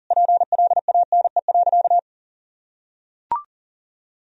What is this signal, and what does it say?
Morse code, heard as a keyed tone, about 40 wpm, 700 Hz. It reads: PLANE.
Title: Sentences from Top 1000 Words - Review Part 3 - 40wpm